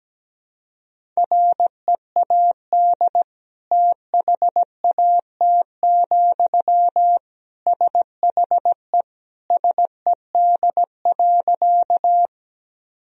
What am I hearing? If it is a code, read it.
Morse code, heard as a keyed tone, about 17 wpm, 700 Hz. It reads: READ THAT, SHE SED.
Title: Sentences from Top 200 Words and Abbreviations 17wpm